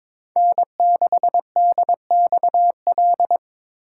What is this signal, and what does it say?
Morse code, heard as a keyed tone, about 22 wpm, 700 Hz. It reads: N6DXL
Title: USA Call Signs - 5 Characters - 4x Character Spacing - 22wpm